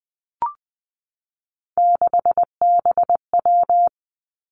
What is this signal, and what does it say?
Morse code, heard as a keyed tone, about 20 wpm, 700 Hz. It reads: 6BW
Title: Three Challenging Letter-Number Combinations 20wpm